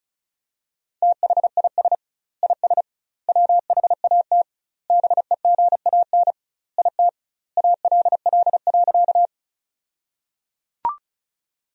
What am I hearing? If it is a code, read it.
Morse code, heard as a keyed tone, about 34 wpm, 700 Hz. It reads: THIS IS WHAT BEGAN IT ALL.
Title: 2x Word-Spacing - Sentences from Top 300 Words - 35wpm